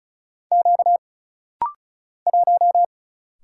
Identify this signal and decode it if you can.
Morse code, heard as a keyed tone, about 35 wpm, 700 Hz. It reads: Q 1
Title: Single Letter-Number - Rapid-Fire 35wpm